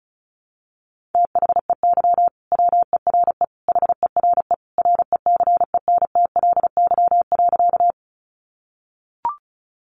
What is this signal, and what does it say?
Morse code, heard as a keyed tone, about 34 wpm, 700 Hz. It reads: THEY WERE HERE RECENTLY.
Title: Sentences from Top 1200 Words 35wpm